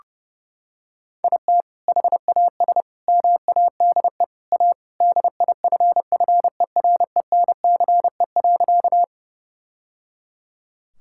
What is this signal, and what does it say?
Morse code, heard as a keyed tone, about 30 wpm, 700 Hz. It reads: IT HAS MADE A DIFFERENCE.
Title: Sentences from Top 1000 Words - Review Part 5 - 30wpm